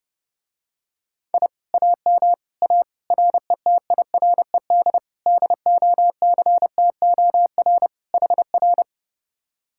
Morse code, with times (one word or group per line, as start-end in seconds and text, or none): 1.3-1.5 I
1.7-2.3 AM
2.6-2.8 A
3.1-5.0 RETIRED
5.3-7.9 DOCTOR
8.1-8.8 HR